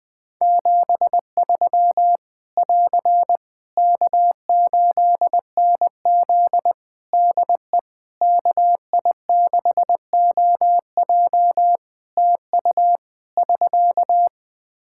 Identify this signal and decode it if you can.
Morse code, heard as a keyed tone, about 20 wpm, 700 Hz. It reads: 73 <AR> K8NZ DE KI6OJ TU <SK>
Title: Practice QSO - Exchange 4 - Ending - 20wpm